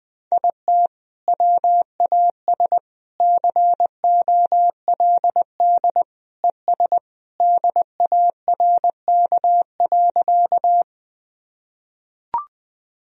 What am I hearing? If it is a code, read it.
Morse code, heard as a keyed tone, about 21 wpm, 700 Hz. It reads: IT WAS COLD ES DARK.
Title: Sentences from Top 500 Words and Abbreviations 20wpm